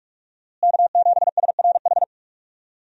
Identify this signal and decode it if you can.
Morse code, heard as a keyed tone, about 46 wpm, 700 Hz. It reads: K7SRH
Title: USA Call Signs - 5 Characters - Repeated 3x - 45wpm